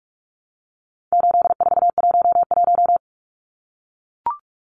Morse code, heard as a keyed tone, about 44 wpm, 700 Hz. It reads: 8411
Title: Numbers - 4-Digit 45wpm